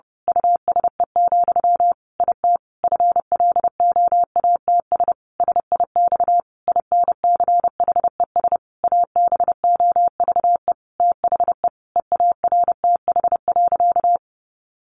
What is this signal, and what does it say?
Morse code, heard as a keyed tone, about 30 wpm, 700 Hz. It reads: USE, IT FLOATS SIX INCHES ABOVE THE EARTH.
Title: A Sound Of Thunder 30wpm